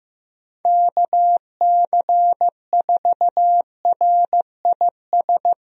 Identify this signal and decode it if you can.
Morse code, heard as a keyed tone, about 16 wpm, 700 Hz. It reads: KC4RIS
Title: USA Call Signs - 6 Characters - Repeated 3x - 15wpm